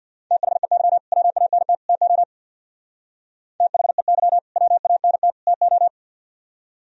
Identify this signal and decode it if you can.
Morse code, heard as a keyed tone, about 65 wpm, 700 Hz. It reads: THEY WANT TO THEY WANT TO